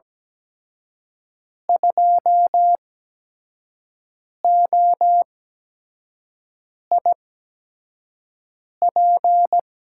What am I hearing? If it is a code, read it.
Morse code, heard as a keyed tone, about 17 wpm, spaced as for 5 wpm, 700 Hz. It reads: 2OIP
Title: USA Call Signs - 5 Characters - 8x Character Spacing - 17wpm